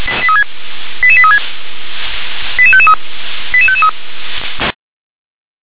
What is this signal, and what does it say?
Five-tone selective call (ZVEI), heard as quick runs of five tones, tones of about 70 ms. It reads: 88836 88553 88553